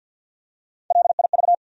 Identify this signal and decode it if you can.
Morse code, heard as a keyed tone, about 53 wpm, 700 Hz. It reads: LIV
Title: USA Common Names - 50wpm